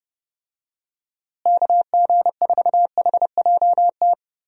Morse code, heard as a keyed tone, about 30 wpm, 700 Hz. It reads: KG4HJT
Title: USA Call Signs - 6 Characters - Repeated 3x - 30wpm